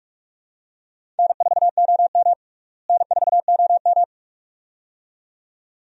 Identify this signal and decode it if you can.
Morse code, heard as a keyed tone, about 47 wpm, 700 Hz. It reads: N4OM N4OM
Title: USA Call Signs - 4 Characters - Repeated 2x - 45wpm